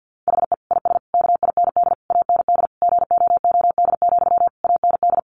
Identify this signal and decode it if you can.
Morse code, heard as a keyed tone, about 50 wpm, 700 Hz. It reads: HE IS KIND AND GOOD, AND
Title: Sentences from Top 1400 Words 50wpm